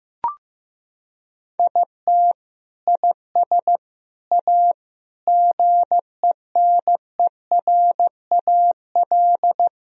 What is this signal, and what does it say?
Morse code, heard as a keyed tone, about 16 wpm, 700 Hz. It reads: IT IS A GENERAL
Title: Sentences from Top 1000 Words - Review Part 3 - 15wpm